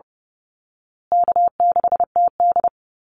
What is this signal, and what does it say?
Morse code, heard as a keyed tone, about 30 wpm, 700 Hz. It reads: K6TD
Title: Call Signs Repeated 3x - 30wpm